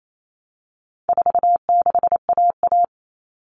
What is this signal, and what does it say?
Morse code, heard as a keyed tone, about 28 wpm, 700 Hz. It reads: 46AA
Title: All Cut Numbers - 4-Digit 28wpm